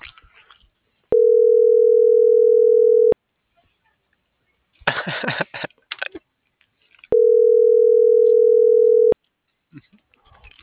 Ringback tone (North America), 440 + 480 Hz.